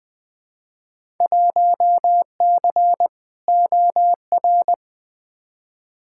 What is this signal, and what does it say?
Morse code, heard as a keyed tone, about 20 wpm, 700 Hz. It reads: 1C OR
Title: Field Day Exchanges - 20wpm